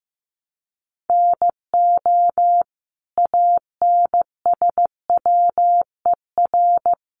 Morse code, heard as a keyed tone, about 15 wpm, 700 Hz. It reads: NO ANSWER
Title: Sentences from Top 700 Words 15wpm